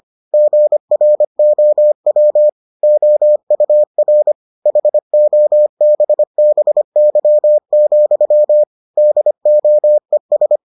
Morse code, heard as a keyed tone, about 25 wpm, 600 Hz. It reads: GROW OUR HOBBY, DOES